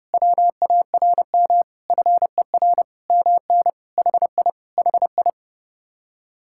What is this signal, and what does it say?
Morse code, heard as a keyed tone, about 31 wpm, 700 Hz. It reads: WARM FER MN HI HI